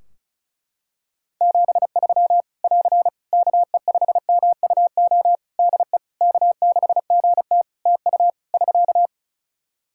Morse code, heard as a keyed tone, about 36 wpm, 700 Hz. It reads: 73 <AR> KE5MUO DE K6GT TU <SK>
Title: Practice QSO - Exchange 4 - Ending - 35wpm